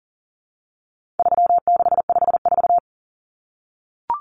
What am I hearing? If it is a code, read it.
Morse code, heard as a keyed tone, about 40 wpm, 700 Hz. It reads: 3654
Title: Common Cut Numbers - 4-Digit 40wpm